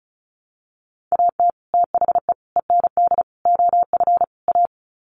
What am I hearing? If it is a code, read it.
Morse code, heard as a keyed tone, about 34 wpm, 700 Hz. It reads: AT THE END OF A
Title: Sets of 5 Words 35wpm